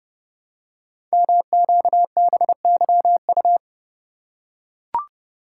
Morse code, heard as a keyed tone, about 30 wpm, 700 Hz. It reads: MQBYU
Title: Code Groups - 5 Letters - 30wpm